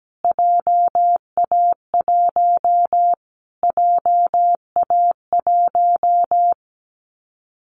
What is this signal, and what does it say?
Morse code, heard as a keyed tone, about 17 wpm, 700 Hz. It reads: JA1 JA1
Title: Callsign Prefixes for Top 25 DXCC - Repeated 3x - 17wpm